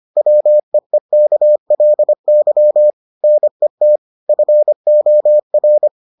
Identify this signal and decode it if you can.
Morse code, heard as a keyed tone, about 25 wpm, 600 Hz. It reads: WEEKLY NET FOR